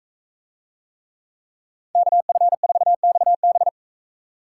Morse code, heard as a keyed tone, about 42 wpm, 700 Hz. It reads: KF4XB